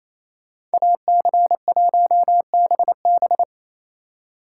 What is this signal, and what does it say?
Morse code, heard as a keyed tone, about 33 wpm, 700 Hz. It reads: AC1BB
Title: Speed Racing - USA 2x2 Call Signs 28wpm